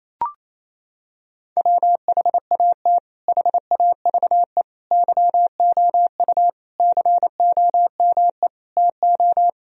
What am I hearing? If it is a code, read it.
Morse code, heard as a keyed tone, about 28 wpm, 700 Hz. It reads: WHAT HAVE YOU COME TO
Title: Sentences from Top 1000 Words - Review Part 2 - 28wpm